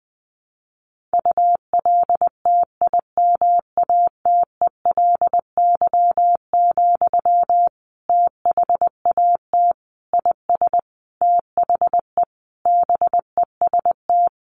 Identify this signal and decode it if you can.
Morse code, heard as a keyed tone, about 19 wpm, 700 Hz. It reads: ULTIMATELY, THAT IS THE BEST
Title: Sentences from Top 1200 Words 20wpm